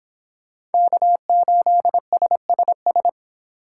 Morse code, heard as a keyed tone, about 29 wpm, 700 Hz. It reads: K8SSS